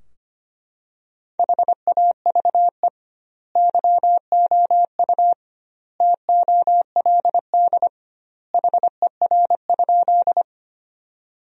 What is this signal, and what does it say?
Morse code, heard as a keyed tone, about 25 wpm, 700 Hz. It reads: HAVE YOU TOLD HER?
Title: 2x Word-Spacing - Sentences from Top 400 Words - 25wpm